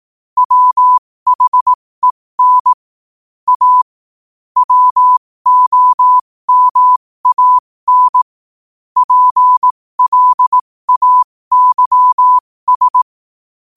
Morse code, heard as a keyed tone, about 18 wpm, spaced as for 15 wpm, 1000 Hz. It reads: WHEN A WOMAN PLAYS
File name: qotdfiles / 15wpm